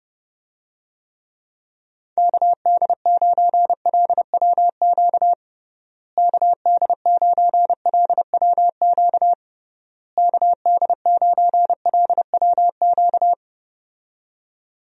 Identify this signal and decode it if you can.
Morse code, heard as a keyed tone, about 30 wpm, 700 Hz. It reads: KD9LWQ KD9LWQ KD9LWQ
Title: USA Call Signs - 6 Characters - Repeated 3x - 30wpm